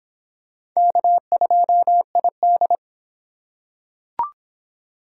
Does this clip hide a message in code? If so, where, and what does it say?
Yes: Morse code from 0.8 to 2.8 s, heard as a keyed tone, about 25 wpm, 700 Hz. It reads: K2ID